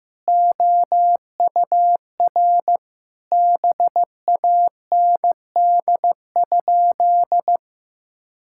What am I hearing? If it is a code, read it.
Morse code, heard as a keyed tone, about 16 wpm, 700 Hz. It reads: OUR BAND?
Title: Sentences from Top 1600 Words 15wpm